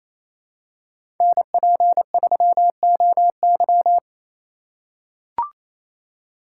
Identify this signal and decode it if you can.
Morse code, heard as a keyed tone, about 28 wpm, 700 Hz. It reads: NP3OY
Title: USA Call Signs - 2x Character Spacing - 28wpm